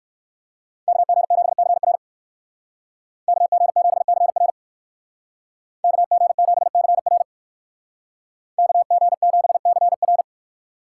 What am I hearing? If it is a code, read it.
Morse code, heard as a keyed tone, about 58 wpm, 700 Hz. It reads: KG7CR KG7CR KG7CR KG7CR